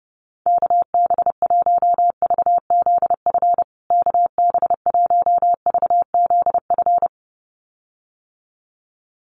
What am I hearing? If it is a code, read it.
Morse code, heard as a keyed tone, about 30 wpm, 700 Hz. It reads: KB1VZF KB1VZF
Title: USA Call Signs Repeated 2x 30wpm